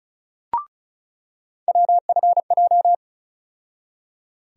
Morse code, heard as a keyed tone, about 36 wpm, 700 Hz. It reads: WFJ
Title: USA Call Sign Postfixes - 3 Letters 35wpm